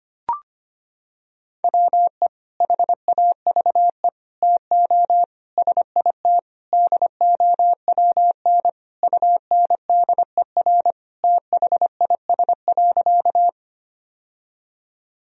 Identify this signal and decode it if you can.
Morse code, heard as a keyed tone, about 25 wpm, 700 Hz. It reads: WE HAVE TO SIT DOWN UNDER THIS.
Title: Sentences from Top 1000 Words - Review Part 1 - 25wpm